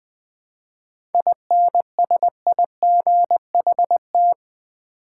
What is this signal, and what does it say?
Morse code, heard as a keyed tone, about 20 wpm, 700 Hz. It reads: INSIGHT
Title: 7 Letter Words - First 3 Letters Given - 20wpm